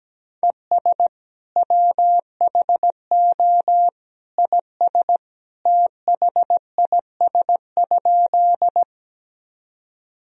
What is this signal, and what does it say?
Morse code, heard as a keyed tone, about 17 wpm, 700 Hz. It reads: ES WHO IS THIS?
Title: Sentences from Top 100 Words and Abbreviations 17wpm